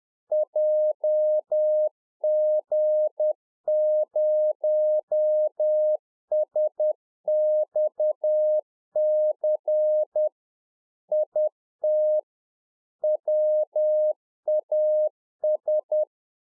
Morse code, heard as a keyed tone, about 10 wpm, 610 Hz. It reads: JG0SXC IT WAS